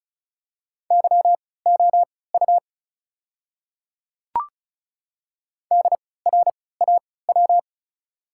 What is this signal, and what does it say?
Morse code, heard as a keyed tone, about 36 wpm, spaced as for 20 wpm, 690 Hz. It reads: YOU DRAW
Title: 3x Character Spacing - Top 300 Words - 35wpm